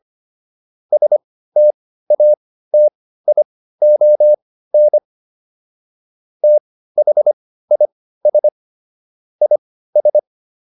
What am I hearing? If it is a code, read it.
Morse code, heard as a keyed tone, about 25 wpm, spaced as for 15 wpm, 600 Hz. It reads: STATION THIS IS